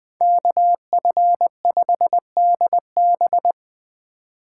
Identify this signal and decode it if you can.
Morse code, heard as a keyed tone, about 20 wpm, 700 Hz. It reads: KF5DB
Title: Speed Racing - USA 2x2 Call Signs 17wpm